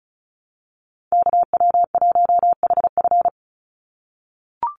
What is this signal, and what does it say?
Morse code, heard as a keyed tone, about 34 wpm, 700 Hz. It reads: KW1HF
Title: USA Call Signs 35wpm